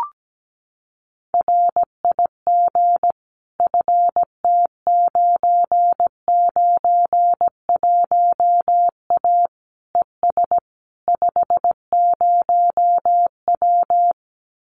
Morse code, heard as a keyed tone, about 17 wpm, 700 Hz. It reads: RIG FT991A ES 50W
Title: Common QSO Phrases Encore 17wpm